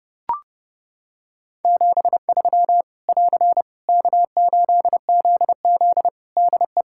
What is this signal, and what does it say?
Morse code, heard as a keyed tone, about 30 wpm, 700 Hz. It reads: 73 <AR> K8ZZ DE
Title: Practice QSO - Exchange 4 - Ending - 30wpm